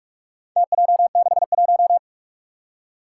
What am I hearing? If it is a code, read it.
Morse code, heard as a keyed tone, about 44 wpm, 700 Hz. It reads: TJ61
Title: Four Challenging Letter-Number Combinations 45wpm